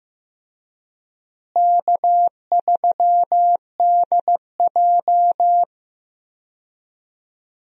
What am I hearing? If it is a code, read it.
Morse code, heard as a keyed tone, about 15 wpm, 700 Hz. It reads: K3DJ
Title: USA Call Signs - 4 Characters - Repeated 2x - 15wpm